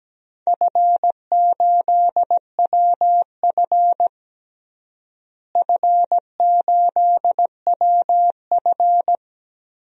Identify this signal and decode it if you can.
Morse code, heard as a keyed tone, about 17 wpm, 700 Hz. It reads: F8WF F8WF
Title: DXCC-100 Call Signs - 4 Characters - Repeated 2x - 17wpm